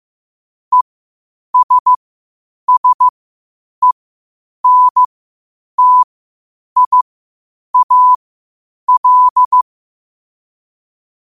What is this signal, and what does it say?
Morse code, heard as a keyed tone, about 15 wpm, spaced as for 8 wpm, 1000 Hz. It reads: ESSENTIAL
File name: qotdfiles / 8wpm